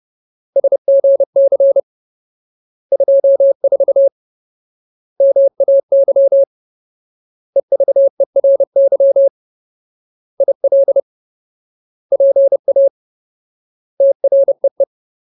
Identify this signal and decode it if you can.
Morse code, heard as a keyed tone, about 30 wpm, 550 Hz. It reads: SGC 24 MAY EVERY IL PA TREE
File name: Ultimate Word List / 30WPM Ultimate